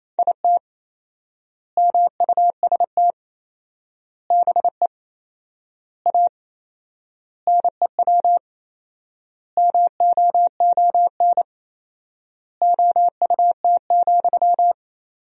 Morse code, heard as a keyed tone, about 28 wpm, 700 Hz. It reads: IT MUST BE A NEW MOON OUT,
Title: 4x Word-Spacing - Sentences from Top 500 Words - 28wpm